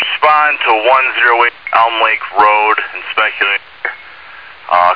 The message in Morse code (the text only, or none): none